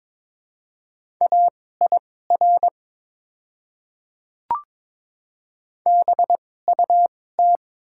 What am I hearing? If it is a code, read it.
Morse code, heard as a keyed tone, about 22 wpm, spaced as for 16 wpm, 700 Hz. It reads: AIR BUT
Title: 2x Character Spacing - Top 200 Words - 22wpm